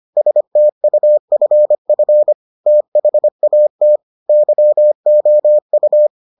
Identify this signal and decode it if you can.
Morse code, heard as a keyed tone, about 25 wpm, 600 Hz. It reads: STUFF THAT YOU